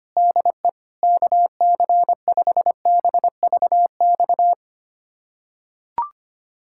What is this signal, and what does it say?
Morse code, heard as a keyed tone, about 25 wpm, 700 Hz. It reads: DE KC5BVX
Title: USA Callsign DE Callsign 25wpm